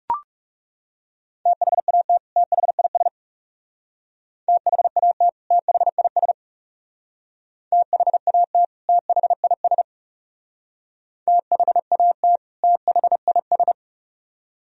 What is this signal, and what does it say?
Morse code, heard as a keyed tone, about 36 wpm, 700 Hz. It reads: THAT THIS THAT THIS THAT THIS THAT THIS
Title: Speed-Racing - Sets of 2 Words 30wpm